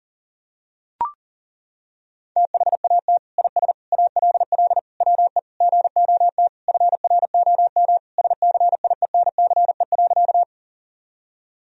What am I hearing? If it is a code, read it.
Morse code, heard as a keyed tone, about 37 wpm, 700 Hz. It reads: THAT IS ALL WE GOT FROM SCIENCE.
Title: Sentences from Top 1000 Words - Review Part 2 - 40wpm